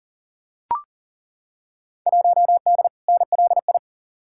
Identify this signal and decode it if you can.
Morse code, heard as a keyed tone, about 40 wpm, 700 Hz. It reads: 1D NLI
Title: Field Day Exchanges - 40wpm